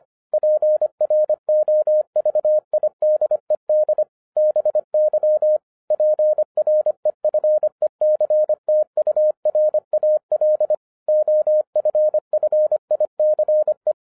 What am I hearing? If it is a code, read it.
Morse code, heard as a keyed tone, about 25 wpm, 610 Hz. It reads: PROVIDED BY PREFECTURAL OFFICE
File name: 2011-10-mp3 / n04-JUST MOVED 25wpm